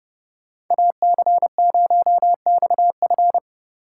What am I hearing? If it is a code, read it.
Morse code, heard as a keyed tone, about 30 wpm, 700 Hz. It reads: AC0XF
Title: Speed Racing - USA 2x2 Call Signs 25wpm